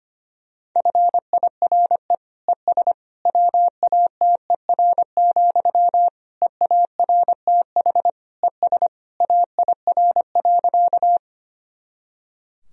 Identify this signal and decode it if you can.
Morse code, heard as a keyed tone, about 25 wpm, 700 Hz. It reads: FIRE ES WATER, EARTH ES AIR.
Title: Sentences from Top 400 Words and Abbreviations 25wpm